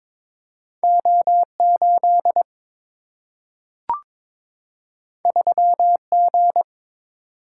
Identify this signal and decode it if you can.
Morse code, heard as a keyed tone, about 22 wpm, 690 Hz. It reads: O8 3G
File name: practice / 22wpm / Two Letter-Number Combinations - 22wpm